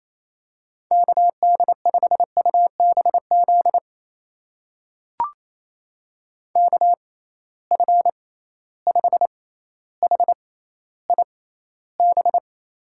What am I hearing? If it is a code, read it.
Morse code, heard as a keyed tone, about 28 wpm, spaced as for 14 wpm, 700 Hz. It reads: KD5UBZ KF5HIB
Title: USA Call Signs - 6 Characters - 6x Character Spacing - 28wpm